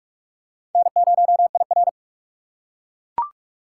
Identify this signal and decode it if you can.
Morse code, heard as a keyed tone, about 38 wpm, 700 Hz. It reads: N0IR